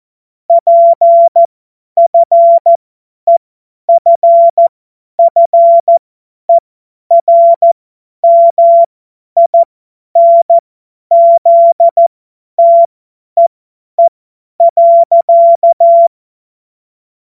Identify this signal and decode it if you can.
Morse code, heard as a keyed tone, about 14 wpm, spaced as for 10 wpm, 680 Hz. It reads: PFEFFERMINZTEE.
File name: Morsekurs / Witze / Witz 30 Januar 2024 50 F70